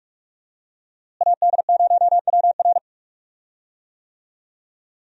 Morse code, heard as a keyed tone, about 41 wpm, 700 Hz. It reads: AD0WR
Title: USA Call Signs - 5 Characters - Repeated 3x - 45wpm